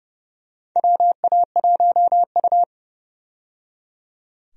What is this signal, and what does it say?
Morse code, heard as a keyed tone, about 30 wpm, 700 Hz. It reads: WA1U
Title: USA Call Signs - 4 Characters - 30wpm